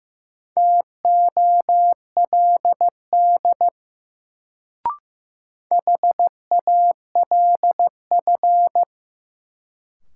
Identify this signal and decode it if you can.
Morse code, heard as a keyed tone, about 15 wpm, 700 Hz. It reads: TOLD HALF
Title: Sentences from Top 400 Words - New Word Drill - 15wpm